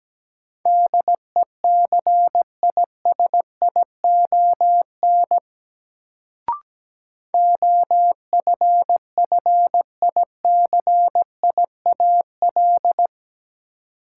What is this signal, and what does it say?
Morse code, heard as a keyed tone, about 17 wpm, 700 Hz. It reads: DECISION OFFICIAL
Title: Sentences from Top 1000 Words - New Word Drill - 17wpm